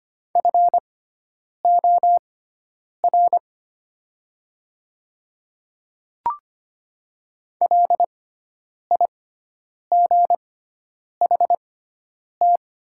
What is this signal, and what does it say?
Morse code, heard as a keyed tone, about 25 wpm, spaced as for 9 wpm, 700 Hz. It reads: FOR LIGHT
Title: 6x Character Spacing - Top 300 Words - 25wpm